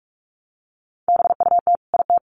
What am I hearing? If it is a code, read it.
Morse code, heard as a keyed tone, about 44 wpm, 700 Hz. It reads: BUT IT
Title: Sets of 2 Words 45wpm